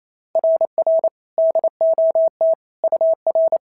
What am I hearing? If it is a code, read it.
Morse code, heard as a keyed tone, about 28 wpm, 690 Hz. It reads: RR DOT UR
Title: SOTA QSO Practice 28wpm